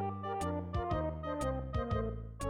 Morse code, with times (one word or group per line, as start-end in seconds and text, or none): none